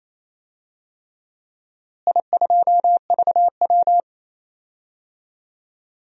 Morse code, heard as a keyed tone, about 28 wpm, 700 Hz. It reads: I2VW
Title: DXCC-100 Call Signs - 4 Characters - Repeated 3x - 28wpm